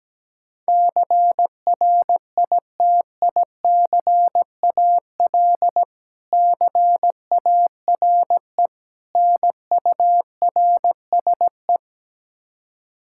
Morse code, heard as a keyed tone, about 17 wpm, 700 Hz. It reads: CRITICAL CARE NURSE
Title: Practice QSO - Exchange 3 - More Info - 17wpm